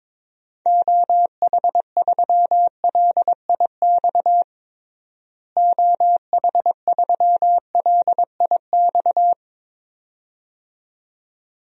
Morse code, heard as a keyed tone, about 21 wpm, 700 Hz. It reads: OH3LIX OH3LIX
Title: DXCC-100 Call Signs - 6 Characters - Repeated 2x - 22wpm